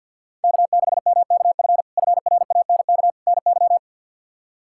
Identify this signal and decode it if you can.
Morse code, heard as a keyed tone, about 53 wpm, 700 Hz. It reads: K6MKF FRANK NY
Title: K1USN SST QSO Practice - 50wpm